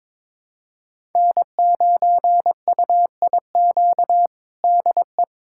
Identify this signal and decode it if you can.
Morse code, heard as a keyed tone, about 22 wpm, 700 Hz. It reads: N9UIQ DE
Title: USA Callsign DE Callsign - 5 Characters - 22wpm